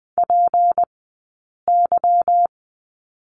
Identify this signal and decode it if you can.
Morse code, heard as a keyed tone, about 20 wpm, spaced as for 8 wpm, 700 Hz. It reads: PY
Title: 151 Lesson 25d - Letter p - Two Characters 20wpm